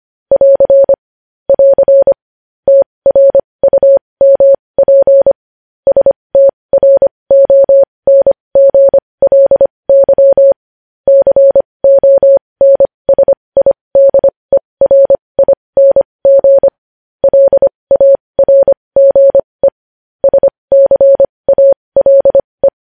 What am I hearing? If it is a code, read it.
Morse code, heard as a keyed tone, about 25 wpm, spaced as for 20 wpm, 550 Hz. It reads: <AR> <AR> TRUMP STRONGLY CONSIDERING LARGE SCALE
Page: news-headlines-in-morse-code-at-20-wpm